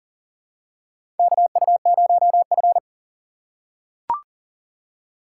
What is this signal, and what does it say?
Morse code, heard as a keyed tone, about 42 wpm, 690 Hz. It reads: KU0F
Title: USA Call Signs - 4 Characters - 2x Character Spacing - 40wpm